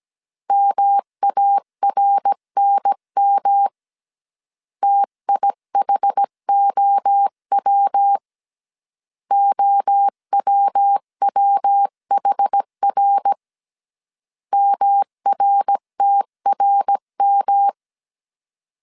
Morse code, heard as a keyed tone, about 17 wpm, 800 Hz. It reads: MARNM TIHOW OWWHR MRTRM
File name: CW / DET / 1 3 1500C S A 16wpm 2141